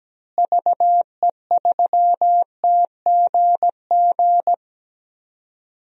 Morse code, heard as a keyed tone, about 17 wpm, 700 Hz. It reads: VE3TGG
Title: DXCC-100 Call Signs - 6 Characters - Repeated 3x - 17wpm